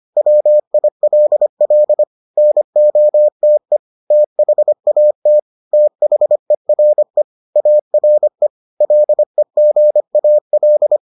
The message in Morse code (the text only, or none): WILL NOTE THAT THERE ARE LEGAL